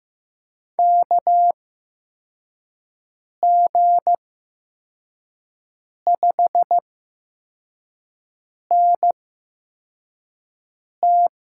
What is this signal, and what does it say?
Morse code, heard as a keyed tone, about 15 wpm, spaced as for 4 wpm, 700 Hz. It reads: KG5NT
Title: USA Call Signs - 6 Characters - 8x Character Spacing - 15wpm